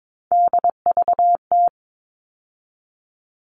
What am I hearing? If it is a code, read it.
Morse code, heard as a keyed tone, about 22 wpm, 700 Hz. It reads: DVT
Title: All Cut Numbers - 3-Digit 22wpm